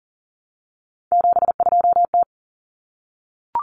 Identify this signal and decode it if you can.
Morse code, heard as a keyed tone, about 40 wpm, 700 Hz. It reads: 72T
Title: Common Cut Numbers - 3-Digit 40wpm